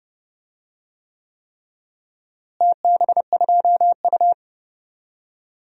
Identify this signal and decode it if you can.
Morse code, heard as a keyed tone, about 31 wpm, 700 Hz. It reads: TB2U
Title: DXCC-100 Call Signs - 4 Characters - Repeated 3x - 30wpm